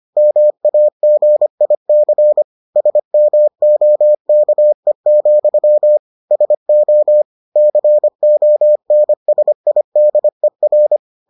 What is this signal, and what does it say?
Morse code, heard as a keyed tone, about 25 wpm, 600 Hz. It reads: MAGIC SMOKE, SO CONSIDER